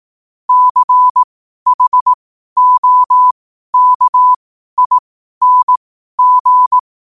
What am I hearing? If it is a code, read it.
Morse code, heard as a keyed tone, about 18 wpm, spaced as for 12 wpm, 1000 Hz. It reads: CHOKING
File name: qotdfiles / 12wpm